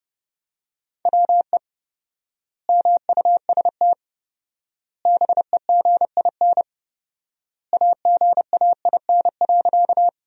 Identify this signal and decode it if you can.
Morse code, heard as a keyed tone, about 32 wpm, 700 Hz. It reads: WE MUST BEGIN AGAIN.
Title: 4x Word-Spacing - Sentences from Top 300 Words - 30wpm